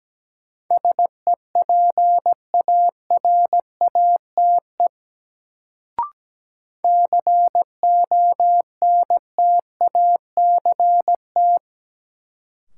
Morse code, heard as a keyed tone, about 17 wpm, 700 Hz. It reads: SEPARATE CONTACT
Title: Sentences from Top 1000 Words - Review Part 7 - New Word Drill - 17wpm